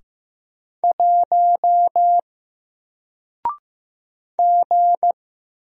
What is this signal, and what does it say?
Morse code, heard as a keyed tone, about 15 wpm, 700 Hz. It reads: 1 G
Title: Single Letter-Number 15wpm